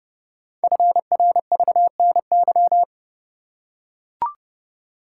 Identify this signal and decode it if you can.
Morse code, heard as a keyed tone, about 28 wpm, 700 Hz. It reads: FRVNY
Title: Code Groups - 5 Letters - 30wpm